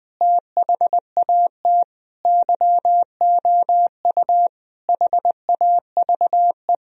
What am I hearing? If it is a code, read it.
Morse code, heard as a keyed tone, about 19 wpm, 700 Hz. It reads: THAT YOU HAVE
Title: Sentences from Top 1000 Words - Review Part 2 - 20wpm